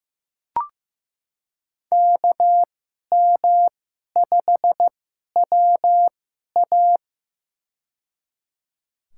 Morse code, heard as a keyed tone, about 15 wpm, spaced as for 11 wpm, 700 Hz. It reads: KM5WA
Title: USA Call Signs - 2x Character Spacing - 15wpm